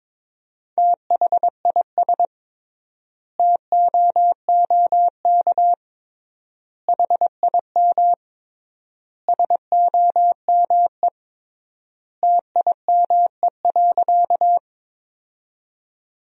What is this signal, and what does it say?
Morse code, heard as a keyed tone, about 22 wpm, 700 Hz. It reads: THIS TOOK HIM SOME TIME.
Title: 3x Word-Spacing - Sentences from Top 300 Words - 22wpm